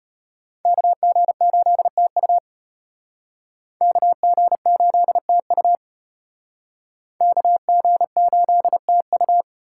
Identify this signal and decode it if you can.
Morse code, heard as a keyed tone, about 34 wpm, 700 Hz. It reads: KG8TU KG8TU KG8TU